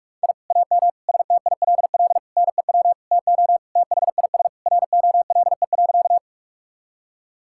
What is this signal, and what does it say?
Morse code, heard as a keyed tone, about 44 wpm, 700 Hz. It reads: I AM STILL NEW TO THIS ROLE.
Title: Sentences from Top 1000 Words - Review Part 6 - 45wpm